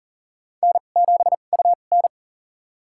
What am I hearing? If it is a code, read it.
Morse code, heard as a keyed tone, about 39 wpm, spaced as for 28 wpm, 690 Hz. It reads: N7UN
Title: USA Call Signs - 4 Characters - 2x Character Spacing - 40wpm